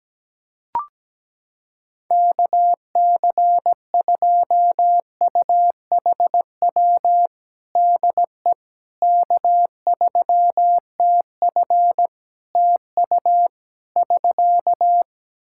Morse code, heard as a keyed tone, about 17 wpm, 700 Hz. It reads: KC2UHW DE K3TF TU <SK>
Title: Practice QSO - Exchange 4 - Ending - 17wpm